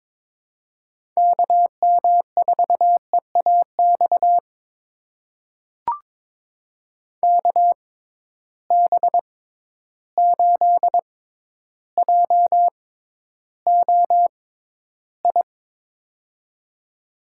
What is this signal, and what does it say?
Morse code, heard as a keyed tone, about 22 wpm, spaced as for 11 wpm, 700 Hz. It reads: KM4EAX KB8JOI
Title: USA Call Signs - 6 Characters - 6x Character Spacing - 22wpm